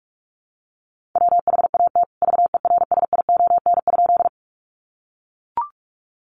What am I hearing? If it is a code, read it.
Morse code, heard as a keyed tone, about 44 wpm, 700 Hz. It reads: WHAT VERSION?
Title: Sentences from Top 1400 Words 45wpm